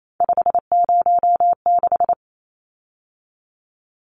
Morse code, heard as a keyed tone, about 28 wpm, 700 Hz. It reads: 506
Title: Numbers - 3-Digit 28wpm